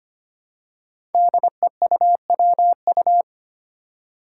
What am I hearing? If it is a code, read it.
Morse code, heard as a keyed tone, about 25 wpm, 700 Hz. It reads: DEUWU
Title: Five Challenging Letter-Number Combinations 25wpm